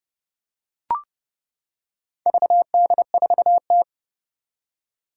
Morse code, heard as a keyed tone, about 32 wpm, 700 Hz. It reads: VD4T
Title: Four Challenging Letter-Number Combinations 30wpm